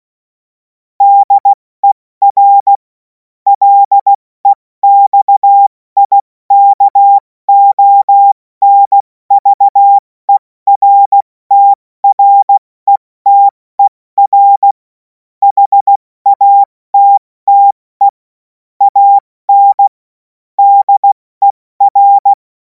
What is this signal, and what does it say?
Morse code, heard as a keyed tone, about 16 wpm, 800 Hz. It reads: DER LEXIKONVERTRETER HATTE AN DER
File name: Morsekurs / Witze / Witz 24. März 2026 70 F80